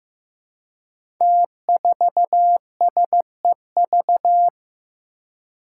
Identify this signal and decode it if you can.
Morse code, heard as a keyed tone, about 15 wpm, 700 Hz. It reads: T4SEV
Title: Five Challenging Letter-Number Combinations 15wpm